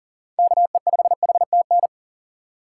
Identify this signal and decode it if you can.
Morse code, heard as a keyed tone, about 40 wpm, 700 Hz. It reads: KE5HTN
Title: USA Call Signs - 6 Characters - 4x Character Spacing - 40wpm